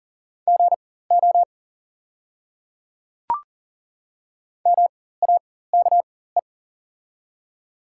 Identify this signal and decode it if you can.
Morse code, heard as a keyed tone, about 40 wpm, spaced as for 19 wpm, 700 Hz. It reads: GO MAKE
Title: 4x Character Spacing - Top 100 Words - 40wpm